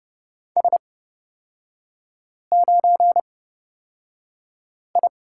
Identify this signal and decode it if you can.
Morse code, heard as a keyed tone, about 30 wpm, spaced as for 5 wpm, 700 Hz. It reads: S9I
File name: Lesson 36e - BT - Three Characters 30wpm